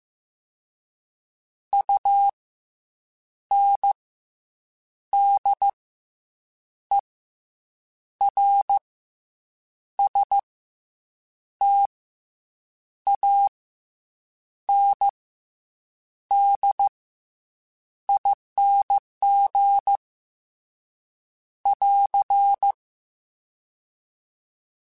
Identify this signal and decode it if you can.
Morse code, heard as a keyed tone, about 15 wpm, 800 Hz. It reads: U N D E R S T A N D ING <AR>